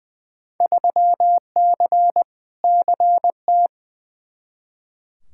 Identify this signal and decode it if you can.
Morse code, heard as a keyed tone, about 20 wpm, 700 Hz. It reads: 3C CT
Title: Field Day Exchanges Encore - 20wpm